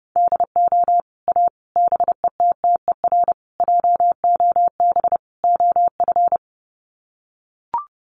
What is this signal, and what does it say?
Morse code, heard as a keyed tone, about 30 wpm, 700 Hz. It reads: DO A BETTER JOB OF
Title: Sets of 5 Words Encore2 30wpm